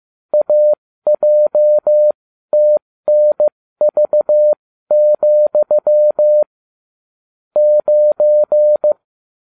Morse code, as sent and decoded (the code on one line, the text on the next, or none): .- .--- - -. ...- --..-- / ----.
AJTNV, 9